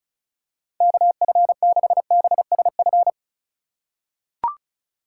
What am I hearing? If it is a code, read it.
Morse code, heard as a keyed tone, about 36 wpm, 700 Hz. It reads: KF6BSF
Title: USA Call Signs - 3x Character Spacing - 35wpm